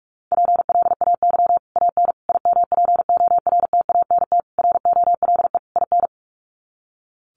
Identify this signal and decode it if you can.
Morse code, heard as a keyed tone, about 44 wpm, 700 Hz. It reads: PLAY AN IMPORTANT ROLE IN
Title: Sets of 5 Words Encore 45wpm